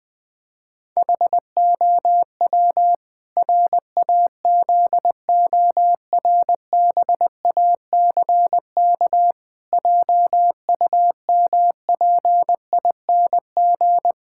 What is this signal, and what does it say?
Morse code, heard as a keyed tone, about 21 wpm, 700 Hz. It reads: HOW RAZORBACK JUMPING
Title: Pangram Challenge v2 - 20wpm